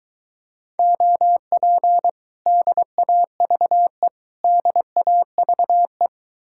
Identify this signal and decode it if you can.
Morse code, heard as a keyed tone, about 23 wpm, 700 Hz. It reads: OP DAVE DAVE